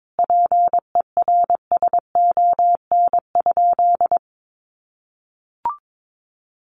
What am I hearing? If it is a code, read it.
Morse code, heard as a keyed tone, about 22 wpm, 700 Hz. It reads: PERSON?
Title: Sentences from Top 1400 Words 22wpm